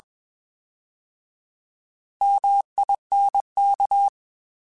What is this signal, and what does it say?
Morse code, heard as a keyed tone, about 21 wpm, 800 Hz. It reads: MINK
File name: Alphabetical / Alphabetical mp3s / Ms @20